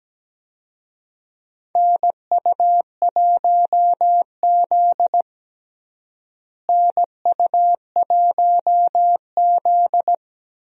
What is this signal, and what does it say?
Morse code, heard as a keyed tone, about 17 wpm, 700 Hz. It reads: NU1Z NU1Z
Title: USA Call Signs - 4 Characters - Repeated 2x - 17wpm